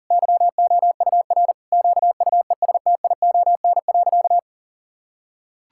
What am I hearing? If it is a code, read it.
Morse code, heard as a keyed tone, about 40 wpm, 700 Hz. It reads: YOUR QUESTION.
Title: Sentences from Top 1000 Words - Review Part 6 - 40wpm